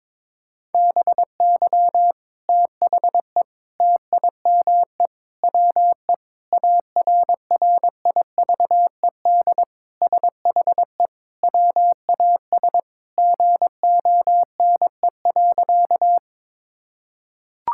Morse code, heard as a keyed tone, about 22 wpm, 700 Hz. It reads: BY THE TIME WE ARRIVED SHE WAS GONE.
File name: practice / 22wpm / Sentences with 3-Word Phrases - 22wpm